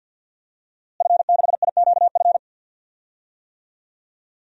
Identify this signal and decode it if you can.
Morse code, heard as a keyed tone, about 46 wpm, 700 Hz. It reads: F6IQF
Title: DXCC-100 Call Signs - 5 Characters - Repeated 2x - 50wpm